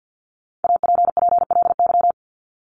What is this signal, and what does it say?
Morse code, heard as a keyed tone, about 50 wpm, 700 Hz. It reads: APPLY
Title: ED Suffix - Top 100 Words with Base Words 50wpm